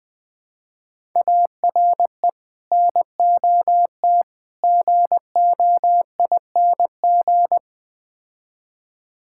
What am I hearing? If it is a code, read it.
Morse code, heard as a keyed tone, about 20 wpm, 700 Hz. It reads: ARE NOT GOING
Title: Build A Phrase - 4-Word Phrases - 20wpm